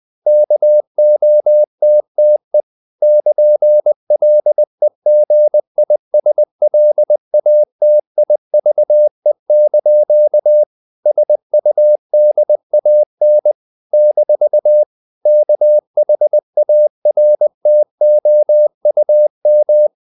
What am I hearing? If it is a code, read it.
Morse code, heard as a keyed tone, about 20 wpm, 600 Hz. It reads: KOTTE <KN>LEGISLATIVE) SUDAN - KHARTOUM
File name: CW / Practice / NationalCapitals20